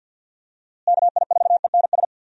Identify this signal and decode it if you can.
Morse code, heard as a keyed tone, about 49 wpm, 700 Hz. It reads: KI4ENS
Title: USA Call Signs - 6 Characters - Repeated 3x - 50wpm